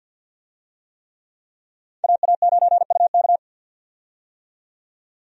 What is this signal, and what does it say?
Morse code, heard as a keyed tone, about 47 wpm, 700 Hz. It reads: AA9UK
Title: USA Call Signs - 5 Characters - 50wpm